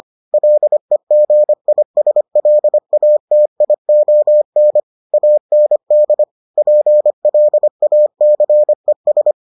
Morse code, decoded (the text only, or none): LEGISLATION AND PLACES